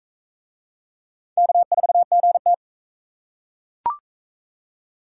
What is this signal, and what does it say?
Morse code, heard as a keyed tone, about 37 wpm, 700 Hz. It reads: K4GT